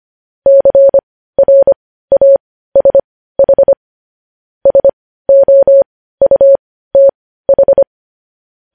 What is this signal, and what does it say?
Morse code, heard as a keyed tone, about 25 wpm, spaced as for 15 wpm, 550 Hz. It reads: CRASH SOUTH
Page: News Headlines in Morse Code at 15 WPM